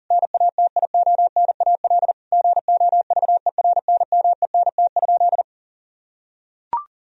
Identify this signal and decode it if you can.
Morse code, heard as a keyed tone, about 40 wpm, 700 Hz. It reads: NATIONAL GOVERNMENT?
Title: Sentences from Top 1000 Words - Review Part 4 - 40wpm